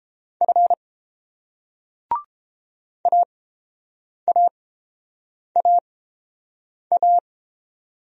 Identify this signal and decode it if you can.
Morse code, heard as a keyed tone, about 30 wpm, spaced as for 7 wpm, 700 Hz. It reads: F AAAA